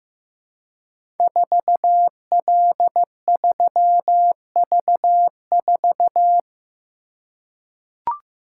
Morse code, heard as a keyed tone, about 15 wpm, 700 Hz. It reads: 4L3V4
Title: Code Groups - 5 Letter:Numbers - 15wpm